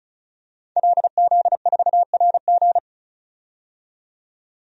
Morse code, heard as a keyed tone, about 36 wpm, 700 Hz. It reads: LZ4RG
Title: DXCC-100 Call Signs - 5 Characters - 35wpm